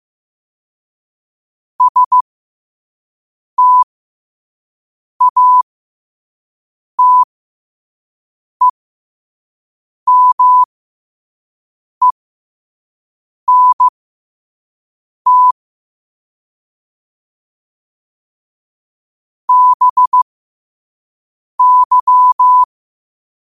Morse code, heard as a keyed tone, about 15 wpm, spaced as for 5 wpm, 1000 Hz. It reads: STATEMENT BY